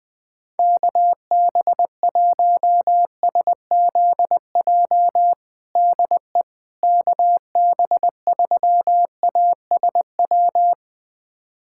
Morse code, heard as a keyed tone, about 20 wpm, 700 Hz. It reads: KB1SZJ DE KB3ASW
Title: USA Callsign DE Callsign - 6 Characters - 20wpm